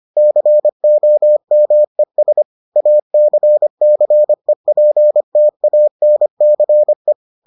Morse code, heard as a keyed tone, about 25 wpm, 600 Hz. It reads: COMES ACCEPTANCE